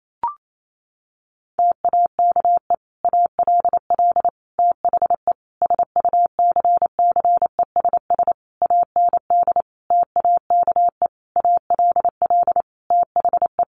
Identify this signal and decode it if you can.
Morse code, heard as a keyed tone, about 28 wpm, 700 Hz. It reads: TAKE ALL THE SUCCESS AND TAKE ALL THE
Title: Sentences from Top 900 Words 28wpm